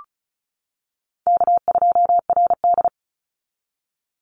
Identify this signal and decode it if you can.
Morse code, heard as a keyed tone, about 36 wpm, 700 Hz. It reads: K2RD
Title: Call Signs - 35wpm